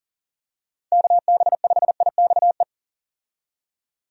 Morse code, heard as a keyed tone, about 38 wpm, 700 Hz. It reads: KB5IXE
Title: USA Call Signs - 6 Characters - 40wpm